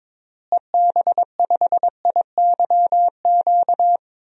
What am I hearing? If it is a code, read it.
Morse code, heard as a keyed tone, about 22 wpm, 700 Hz. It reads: EB5IYQ
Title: DXCC-100 Call Signs - 6 Characters - Repeated 2x - 22wpm